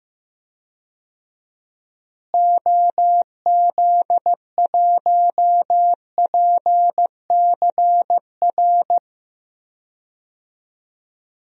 Morse code, heard as a keyed tone, about 15 wpm, 700 Hz. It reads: OZ1PCR
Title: DXCC-100 Call Signs - 6 Characters - Repeated 2x - 15wpm